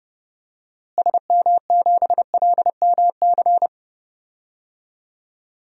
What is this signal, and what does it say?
Morse code, heard as a keyed tone, about 30 wpm, 700 Hz. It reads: SM7LMC
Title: DXCC-100 Call Signs - 6 Characters - Repeated 2x - 30wpm